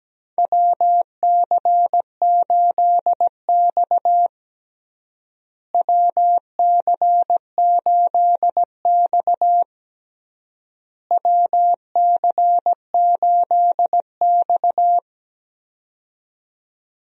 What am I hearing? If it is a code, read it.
Morse code, heard as a keyed tone, about 17 wpm, 700 Hz. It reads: WC8X WC8X WC8X
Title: USA Call Signs - 4 Characters - Repeated 3x - 17wpm